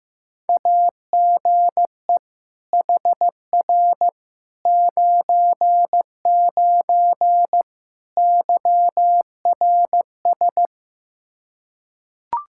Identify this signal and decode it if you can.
Morse code, heard as a keyed tone, about 15 wpm, 700 Hz. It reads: AGE HR 99 YRS
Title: Practice QSO - Exchange 3 - More Info - 15wpm